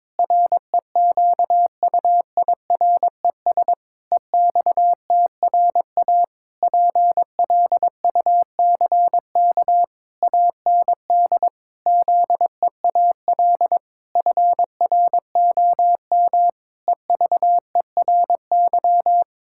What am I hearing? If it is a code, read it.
Morse code, heard as a keyed tone, about 22 wpm, 700 Hz. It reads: REQUIRES EXTRA PLUCK AND ZEAL FROM EVERY
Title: Pangram Challenge v2 - 22wpm